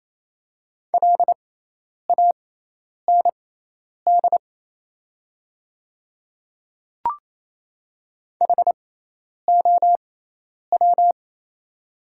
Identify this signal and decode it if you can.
Morse code, heard as a keyed tone, about 27 wpm, spaced as for 10 wpm, 700 Hz. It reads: LAND HOW
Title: 6x Character Spacing - Top 200 Words - 28wpm